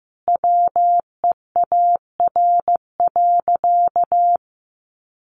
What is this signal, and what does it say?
Morse code, heard as a keyed tone, about 15 wpm, 700 Hz. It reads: WEAR.
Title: Sentences from Top 1500 Words 15wpm